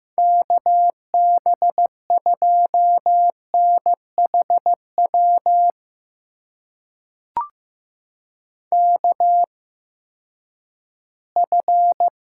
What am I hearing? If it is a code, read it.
Morse code, heard as a keyed tone, about 15 wpm, spaced as for 10 wpm, 700 Hz. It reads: KB2NHW KF
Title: USA Call Signs - 6 Characters - 8x Character Spacing - 15wpm